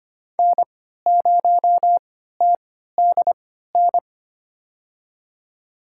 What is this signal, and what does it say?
Morse code, heard as a keyed tone, about 25 wpm, spaced as for 14 wpm, 700 Hz. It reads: N0TDN
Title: USA Call Signs - 5 Characters - 3x Character Spacing - 25wpm